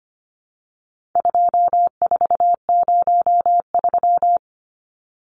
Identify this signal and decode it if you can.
Morse code, heard as a keyed tone, about 25 wpm, 700 Hz. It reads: 2403
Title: Numbers - 4-Digit 25wpm